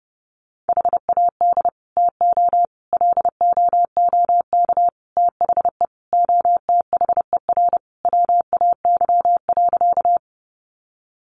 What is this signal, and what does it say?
Morse code, heard as a keyed tone, about 30 wpm, 700 Hz. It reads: HAD TO LOOK THE OTHER WAY.
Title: Sentences from Top 100 Words 30wpm